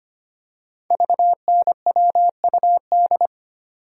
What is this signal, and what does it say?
Morse code, heard as a keyed tone, about 25 wpm, 700 Hz. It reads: VNWUD
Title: Code Groups - 5 Letters - 25wpm